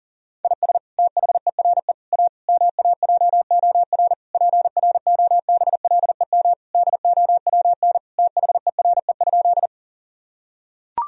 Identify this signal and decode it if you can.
Morse code, heard as a keyed tone, about 40 wpm, 700 Hz. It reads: IS THERE A MAJOR PROBLEM DOWN THERE?
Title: Sentences from Top 1000 Words - Review Part 4 - 40wpm